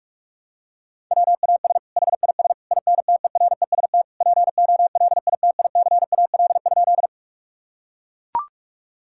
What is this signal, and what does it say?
Morse code, heard as a keyed tone, about 42 wpm, 700 Hz. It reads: WAS HIS INTEREST POLITICAL?
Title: Sentences from Top 1000 Words - Review Part 5 - 45wpm